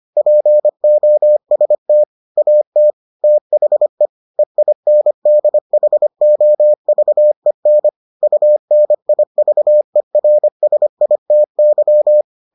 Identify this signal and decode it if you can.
Morse code, heard as a keyed tone, about 25 wpm, 600 Hz. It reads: POST AT THE EINDHOVEN UNIVERSITY